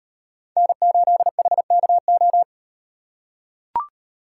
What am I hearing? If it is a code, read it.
Morse code, heard as a keyed tone, about 38 wpm, 700 Hz. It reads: N8HKO